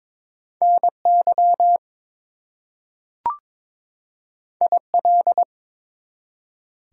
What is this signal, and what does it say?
Morse code, heard as a keyed tone, about 22 wpm, 700 Hz. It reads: NY IL
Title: US States and Canadian Provinces 22wpm